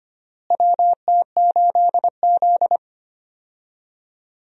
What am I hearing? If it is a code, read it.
Morse code, heard as a keyed tone, about 25 wpm, 700 Hz. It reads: WT8Z
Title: USA Call Signs - 4 Characters - Repeated 3x - 25wpm